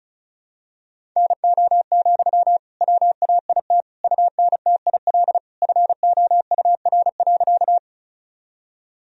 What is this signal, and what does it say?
Morse code, heard as a keyed tone, about 36 wpm, 700 Hz. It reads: NO, WAIT UNTIL FOUR.
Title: Sentences from Top 1000 Words - Review Part 1 - 35wpm